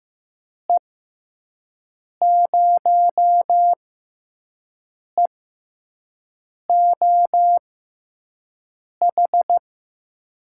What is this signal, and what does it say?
Morse code, heard as a keyed tone, about 15 wpm, spaced as for 5 wpm, 700 Hz. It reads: E0EOH
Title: USA Call Signs - 6 Characters - 6x Character Spacing - 15wpm